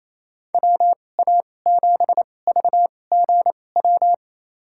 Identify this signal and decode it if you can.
Morse code, heard as a keyed tone, about 28 wpm, spaced as for 20 wpm, 700 Hz. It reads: WA7VGW
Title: USA Call Signs - 6 Characters - 2x Character Spacing - 28wpm